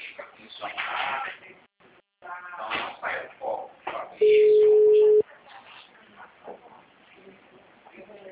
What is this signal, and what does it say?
Ringback tone (Europe, 425 Hz), with speech.